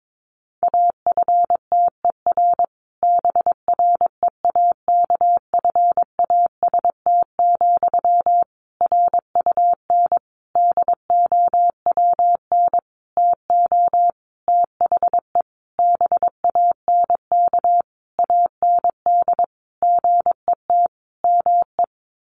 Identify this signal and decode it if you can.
Morse code, heard as a keyed tone, about 22 wpm, 700 Hz. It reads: AFTER BREAKFAST, RUN DOWN TO THE BANK AND GET ME
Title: Sentences from Top 1500 Words 22wpm